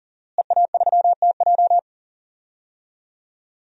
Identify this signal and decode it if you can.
Morse code, heard as a keyed tone, about 40 wpm, 700 Hz. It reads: EA3TJ
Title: DXCC-100 Call Signs - 5 Characters - Repeated 3x - 40wpm